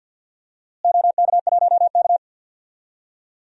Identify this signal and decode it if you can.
Morse code, heard as a keyed tone, about 49 wpm, 700 Hz. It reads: OK1K
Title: DXCC-100 Call Signs - 4 Characters - Repeated 3x - 50wpm